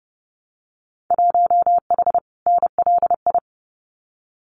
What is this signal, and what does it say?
Morse code, heard as a keyed tone, about 30 wpm, 700 Hz. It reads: 1H NLI
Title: Winter Field Day Exchanges 30wpm